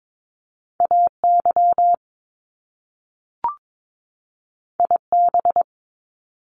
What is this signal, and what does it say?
Morse code, heard as a keyed tone, about 22 wpm, 700 Hz. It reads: AY IB
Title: Top Two Letter Combinations in Words 22wpm